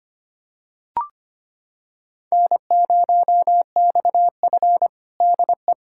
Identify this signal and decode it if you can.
Morse code, heard as a keyed tone, about 25 wpm, 700 Hz. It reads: N0XF DE
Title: USA Callsign DE Callsign - 4 Characters - 25wpm